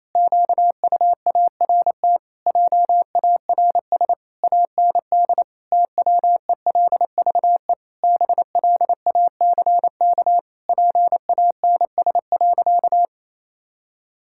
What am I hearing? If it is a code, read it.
Morse code, heard as a keyed tone, about 28 wpm, 700 Hz. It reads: QUART JARS AND TWELVE BLACK PANS.
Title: Pangram Challenge v2 - 28wpm